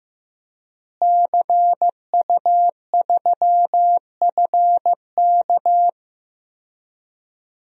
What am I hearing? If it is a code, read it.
Morse code, heard as a keyed tone, about 15 wpm, 700 Hz. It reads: CU3FK
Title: DXCC-100 Call Signs - 5 Characters - Repeated 3x - 15wpm